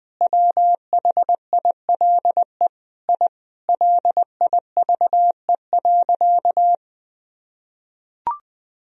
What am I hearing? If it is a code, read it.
Morse code, heard as a keyed tone, about 20 wpm, 700 Hz. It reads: WHILE I LIVE.
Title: Sentences from Top 1000 Words - Review Part 1 - 20wpm